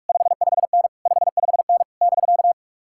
Morse code, heard as a keyed tone, about 59 wpm, spaced as for 51 wpm, 700 Hz. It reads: 55N 55N <BK>
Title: Speed-Racing Common QSO Phrases Encore - 45wpm